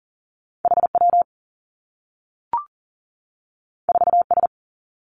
Morse code, heard as a keyed tone, about 40 wpm, 700 Hz. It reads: H W 4 S
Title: Two Challenging Letter-Number Combinations 40wpm